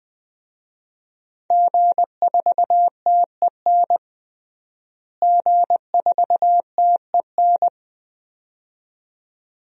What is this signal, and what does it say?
Morse code, heard as a keyed tone, about 21 wpm, 700 Hz. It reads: G4TEN G4TEN
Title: DXCC-100 Call Signs - 5 Characters - Repeated 2x - 20wpm